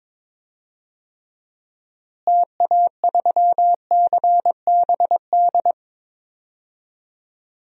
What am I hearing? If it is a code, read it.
Morse code, heard as a keyed tone, about 22 wpm, 700 Hz. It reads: TA3CBD
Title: DXCC-100 Call Signs - 6 Characters - Repeated 3x - 22wpm